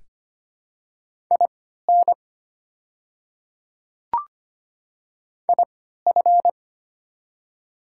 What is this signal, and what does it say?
Morse code, heard as a keyed tone, about 25 wpm, spaced as for 14 wpm, 700 Hz. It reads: IN IF
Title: 3x Character Spacing - Top 200 Words - 25wpm